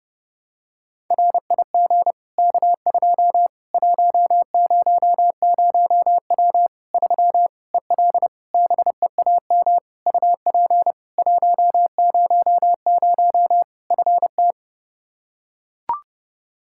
Morse code, heard as a keyed tone, about 30 wpm, 700 Hz. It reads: RIG K2 100W 3 EL BEAM UP 100 FT
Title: K1USN SST QSO Practice 30wpm